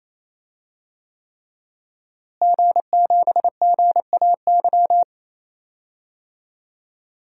Morse code, heard as a keyed tone, about 28 wpm, 700 Hz. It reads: G7GAY
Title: DXCC-100 Call Signs - 5 Characters - Repeated 3x - 28wpm